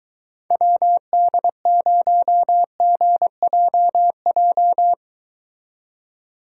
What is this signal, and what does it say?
Morse code, heard as a keyed tone, about 22 wpm, 700 Hz. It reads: WD0GJJ